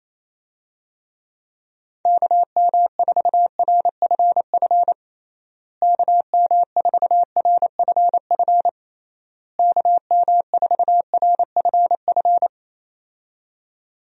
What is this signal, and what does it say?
Morse code, heard as a keyed tone, about 28 wpm, 700 Hz. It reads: KM4RFF KM4RFF KM4RFF
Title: USA Call Signs - 6 Characters - Repeated 3x - 28wpm